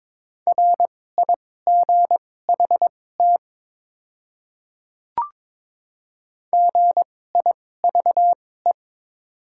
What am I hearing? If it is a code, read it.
Morse code, heard as a keyed tone, about 22 wpm, spaced as for 16 wpm, 690 Hz. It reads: RIGHT GIVE
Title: 2x Character Spacing - Top 500 Words - 22wpm